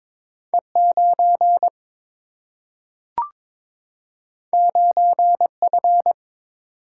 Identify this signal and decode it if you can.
Morse code, heard as a keyed tone, about 22 wpm, 700 Hz. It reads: E 9 9 F
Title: Single-Letter Single-Number 22wpm